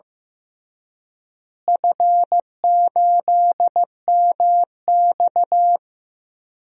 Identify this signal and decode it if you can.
Morse code, heard as a keyed tone, about 15 wpm, 700 Hz. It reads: F8MX
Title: DXCC-100 Call Signs - 4 Characters - Repeated 3x - 15wpm